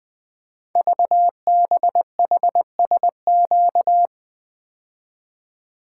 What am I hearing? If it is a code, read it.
Morse code, heard as a keyed tone, about 20 wpm, 700 Hz. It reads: VBHSQ
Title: Code Groups - 5 Letter:Numbers - 20wpm